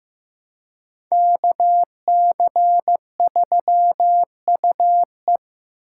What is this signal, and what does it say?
Morse code, heard as a keyed tone, about 18 wpm, 700 Hz. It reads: KC3UE
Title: Speed Racing - USA Call Signs 15wpm